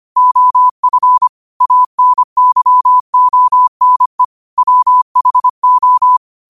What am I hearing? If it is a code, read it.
Morse code, heard as a keyed tone, about 25 wpm, 1000 Hz. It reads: OF ANYONE WHO
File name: qotdfiles / 25wpm